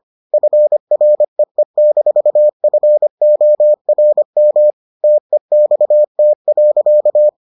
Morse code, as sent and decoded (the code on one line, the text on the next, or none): ..-. .-. . -....- ..-. --- .-. -- / - . -..- - .-.-.-
FREE-FORM TEXT.